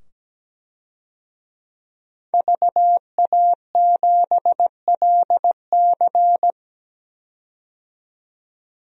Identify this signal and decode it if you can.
Morse code, heard as a keyed tone, about 17 wpm, 700 Hz. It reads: VA7LC
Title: DXCC-100 Call Signs - 5 Characters - 17wpm